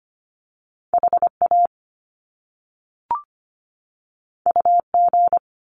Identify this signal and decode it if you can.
Morse code, heard as a keyed tone, about 25 wpm, 700 Hz. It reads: HA UG
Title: Top Two Letter Combinations in Words 25wpm